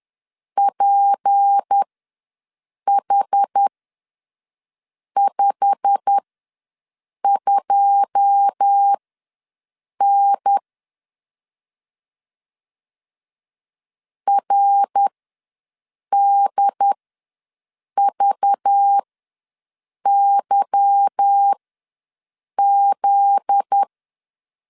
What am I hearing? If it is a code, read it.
Morse code, heard as a keyed tone, about 10 wpm, spaced as for 6 wpm, 800 Hz. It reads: PH52N RDVYZ
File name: LEZIONI / 9 286 ws 10wpm 4032